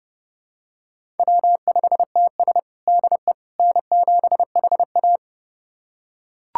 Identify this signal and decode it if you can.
Morse code, heard as a keyed tone, about 30 wpm, 700 Hz. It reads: W5TS DE N7HA
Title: USA Callsign DE Callsign - 4 Characters - 30wpm